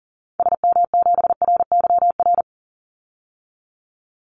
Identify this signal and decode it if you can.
Morse code, heard as a keyed tone, about 40 wpm, 700 Hz. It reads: SM7RYR
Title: Call Signs Repeated 2x - 40wpm